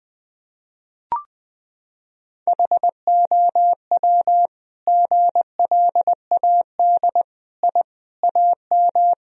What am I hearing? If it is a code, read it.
Morse code, heard as a keyed tone, about 19 wpm, 700 Hz. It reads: HOW GLAD I AM
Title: Sentences from Top 1000 Words - Review Part 2 - 20wpm